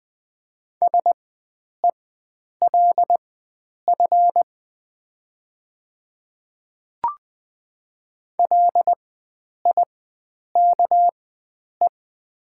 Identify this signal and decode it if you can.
Morse code, heard as a keyed tone, about 20 wpm, spaced as for 9 wpm, 700 Hz. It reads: SELF LIKE
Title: 4x Character Spacing - Top 300 Words - 20wpm